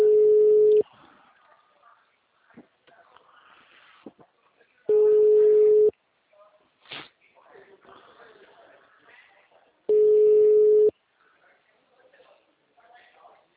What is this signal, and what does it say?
Ringback tone (Europe, 425 Hz), with speech.